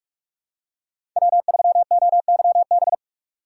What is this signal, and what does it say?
Morse code, heard as a keyed tone, about 46 wpm, 700 Hz. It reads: W3OYB
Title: USA Call Signs - 5 Characters - 3x Character Spacing - 45wpm